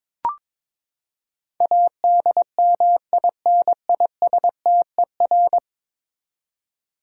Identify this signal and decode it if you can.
Morse code, heard as a keyed tone, about 21 wpm, 700 Hz. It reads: ADMINISTER
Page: morse-code-ninja-practice-22wpm